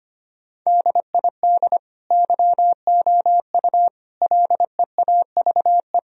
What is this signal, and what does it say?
Morse code, heard as a keyed tone, about 25 wpm, 700 Hz. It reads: DID YOU LEAVE
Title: Sentences from Top 1000 Words - Review Part 4 - 25wpm